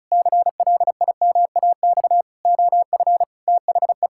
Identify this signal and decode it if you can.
Morse code, heard as a keyed tone, about 34 wpm, 700 Hz. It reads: CLIMAX OF THE
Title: Pangram Challenge v2 - 35wpm